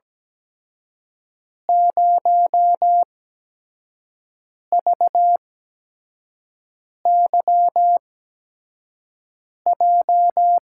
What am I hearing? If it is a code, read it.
Morse code, heard as a keyed tone, about 17 wpm, spaced as for 5 wpm, 700 Hz. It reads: 0VYJ
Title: USA Call Signs - 6 Characters - 8x Character Spacing - 17wpm